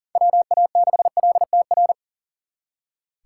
Morse code, heard as a keyed tone, about 52 wpm, 700 Hz. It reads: WA6LTR